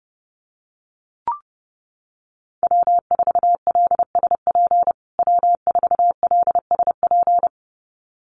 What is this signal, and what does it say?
Morse code, heard as a keyed tone, about 30 wpm, 700 Hz. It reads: W4LSP W4LSP
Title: USA Call Signs Repeated 2x 30wpm